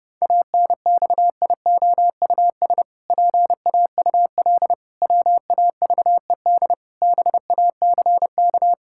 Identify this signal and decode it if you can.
Morse code, heard as a keyed tone, about 32 wpm, 700 Hz. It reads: ANXIOUS PAUL WAVED BACK
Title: Pangram Challenge v2 - 30wpm